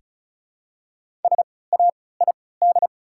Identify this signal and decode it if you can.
Morse code, heard as a keyed tone, about 36 wpm, spaced as for 20 wpm, 700 Hz. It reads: SAID
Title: 3x Character Spacing - Top 100 Words - 35wpm